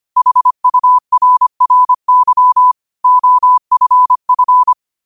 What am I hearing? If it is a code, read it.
Morse code, heard as a keyed tone, about 25 wpm, 1000 Hz. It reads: HURRY OFF
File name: qotdfiles / 25wpm